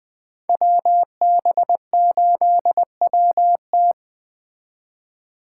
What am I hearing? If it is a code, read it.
Morse code, heard as a keyed tone, about 20 wpm, 700 Hz. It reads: WB8WT